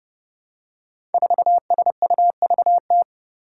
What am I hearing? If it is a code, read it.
Morse code, heard as a keyed tone, about 31 wpm, 700 Hz. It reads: 4SUVT
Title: Five Challenging Letter-Number Combinations 30wpm